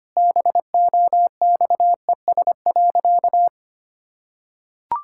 Morse code, heard as a keyed tone, about 25 wpm, 700 Hz. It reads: BOXES.
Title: Pangram Challenge v2 - 25wpm